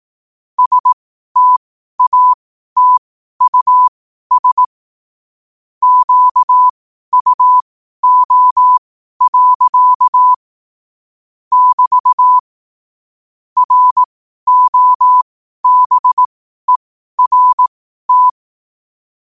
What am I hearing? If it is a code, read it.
Morse code, heard as a keyed tone, about 18 wpm, spaced as for 12 wpm, 1000 Hz. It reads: STATUS QUO. <BT> ROBERT
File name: qotdfiles / 12wpm